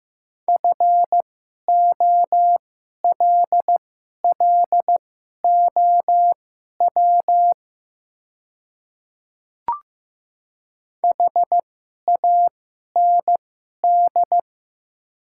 Morse code, heard as a keyed tone, about 15 wpm, spaced as for 11 wpm, 700 Hz. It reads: FOLLOW HAND
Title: 2x Character Spacing - Top 200 Words - 15wpm